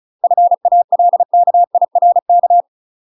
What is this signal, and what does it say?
Morse code, heard as a keyed tone, about 34 wpm, 700 Hz. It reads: FALKIRK
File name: UK MASTER QSOs TRAINING / 35WPM / UK MASTER QSOs 0028